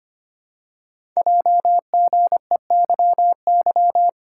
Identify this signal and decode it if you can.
Morse code, heard as a keyed tone, about 25 wpm, 700 Hz. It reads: JGEYY
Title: Code Groups - 5 Letters - 25wpm